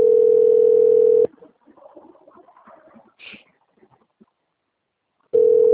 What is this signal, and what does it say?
Ringback tone (North America), 440 + 480 Hz.